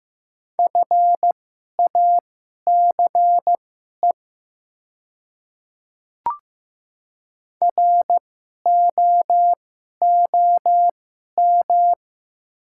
Morse code, heard as a keyed tone, about 15 wpm, spaced as for 11 wpm, 700 Hz. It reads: FACE ROOM
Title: 2x Character Spacing - Top 300 Words - 15wpm